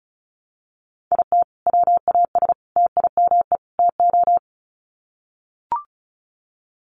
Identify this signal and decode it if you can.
Morse code, heard as a keyed tone, about 34 wpm, 700 Hz. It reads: IT WAS TIME TO
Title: Sets of 4 Words 35wpm